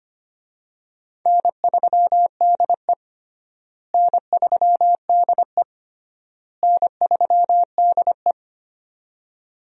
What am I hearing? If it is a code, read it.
Morse code, heard as a keyed tone, about 25 wpm, 700 Hz. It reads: N3DE N3DE N3DE
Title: USA Call Signs - 4 Characters - Repeated 3x - 25wpm